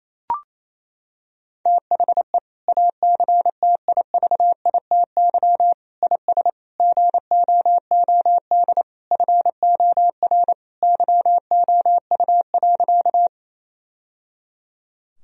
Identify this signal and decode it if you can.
Morse code, heard as a keyed tone, about 28 wpm, 700 Hz. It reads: THE ACTIVITY IS GOOD FOR YOU.
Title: Sentences from Top 1000 Words - Review Part 4 - 28wpm